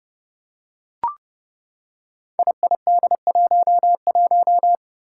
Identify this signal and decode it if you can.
Morse code, heard as a keyed tone, about 29 wpm, 700 Hz. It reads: IID11
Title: Five Challenging Letter-Number Combinations 30wpm